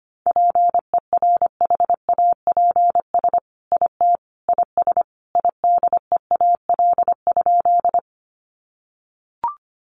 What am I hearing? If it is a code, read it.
Morse code, heard as a keyed tone, about 25 wpm, 700 Hz. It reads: PERHAPS IT IS IDEAL?
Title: Sentences from Top 1400 Words 25wpm